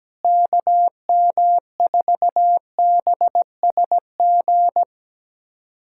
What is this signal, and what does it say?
Morse code, heard as a keyed tone, about 17 wpm, 700 Hz. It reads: KM4BSG
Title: USA Callsign DE Callsign - 6 Characters - 17wpm